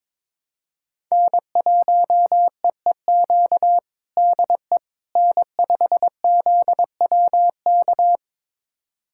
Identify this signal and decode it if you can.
Morse code, heard as a keyed tone, about 22 wpm, 700 Hz. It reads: N1EEQ DE N5ZWK
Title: USA Callsign DE Callsign - 5 Characters - 22wpm